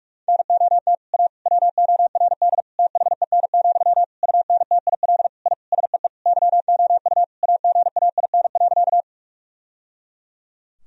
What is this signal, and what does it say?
Morse code, heard as a keyed tone, about 41 wpm, 700 Hz. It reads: NOT A WORD THEN, UNTIL I SEE YOU AGAIN.
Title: Sentences from Top 1000 Words - Review Part 1 - 45wpm